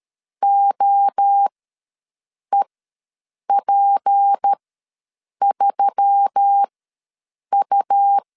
Morse code, heard as a keyed tone, about 13 wpm, spaced as for 7 wpm, 800 Hz. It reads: OEP3U
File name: LEZIONI / 6 PKX0 / 6 PK ws 12wpm 3116